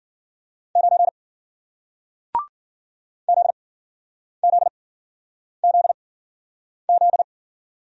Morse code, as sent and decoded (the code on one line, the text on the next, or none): ----. / --.. --.. --.. --..
9 ZZZZ